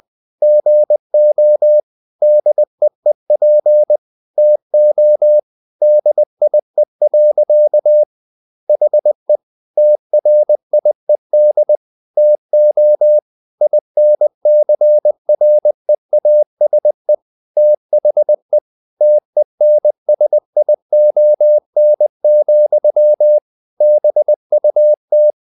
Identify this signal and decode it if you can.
Morse code, heard as a keyed tone, about 20 wpm, 600 Hz. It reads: GO DEEP TO DIE. HE TRIED TO INCREASE THE TENSION, BUT
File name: the old man and the sea / the old man and the sea part 1 20wpm